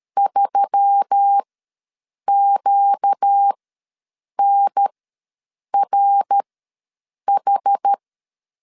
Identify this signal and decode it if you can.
Morse code, heard as a keyed tone, about 13 wpm, spaced as for 7 wpm, 800 Hz. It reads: 3QNRH
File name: LEZIONI / 8 YJ45 / 8 Y ws 12wpm 3409